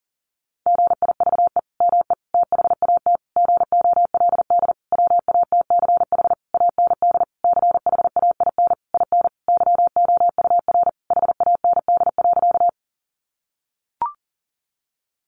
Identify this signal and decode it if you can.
Morse code, heard as a keyed tone, about 37 wpm, 700 Hz. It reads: GIVE ME THAT GOLD WATCH AND CHAIN IN YOUR HAND.
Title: Sentences from Top 1500 Words 40wpm